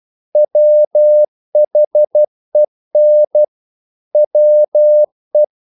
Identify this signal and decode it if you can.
Morse code, heard as a keyed tone, about 12 wpm, 600 Hz. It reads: WHEN WE
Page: English quotations in CW (telegraphy - morse)